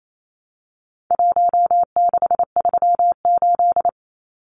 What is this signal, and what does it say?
Morse code, heard as a keyed tone, about 28 wpm, 700 Hz. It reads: 1638
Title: Numbers - Contest Style - 4-Digit 28wpm